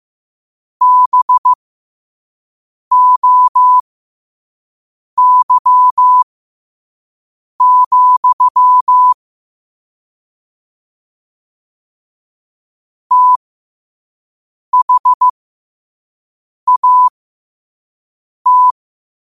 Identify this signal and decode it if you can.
Morse code, heard as a keyed tone, about 15 wpm, spaced as for 5 wpm, 1000 Hz. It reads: BOY, THAT
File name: qotdfiles / 5wpm